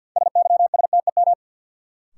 Morse code, heard as a keyed tone, about 51 wpm, 700 Hz. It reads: SYSTEM
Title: Sentences from Top 1000 Words - Review Part 3 - New Word Drill - 50wpm